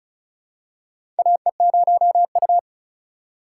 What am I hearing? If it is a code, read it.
Morse code, heard as a keyed tone, about 36 wpm, 700 Hz. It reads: AE0U
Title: USA Call Signs - 4 Characters - 3x Character Spacing - 35wpm